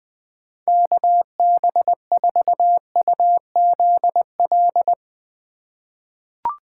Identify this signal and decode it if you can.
Morse code, heard as a keyed tone, about 21 wpm, 700 Hz. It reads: KB4UZL
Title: USA Call Signs - 6 Characters - 4x Character Spacing - 20wpm